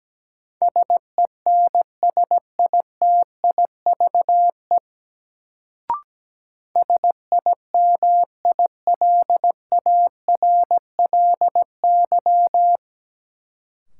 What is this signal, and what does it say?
Morse code, heard as a keyed tone, about 17 wpm, 700 Hz. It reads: SENSITIVE SIMILARLY
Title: Sentences from Top 1400 Words - New Word Drill Alt - 17wpm